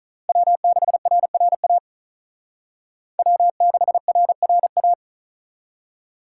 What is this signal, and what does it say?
Morse code, heard as a keyed tone, about 45 wpm, 700 Hz. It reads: W6RRA W6RRA